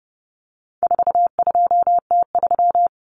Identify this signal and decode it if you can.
Morse code, heard as a keyed tone, about 30 wpm, 700 Hz. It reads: 42T3
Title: Common Cut Numbers - 4-Digit 30wpm